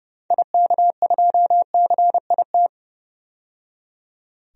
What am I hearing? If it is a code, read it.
Morse code, heard as a keyed tone, about 30 wpm, 700 Hz. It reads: IK2CIT
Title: DXCC-100 Call Signs - 6 Characters - 30wpm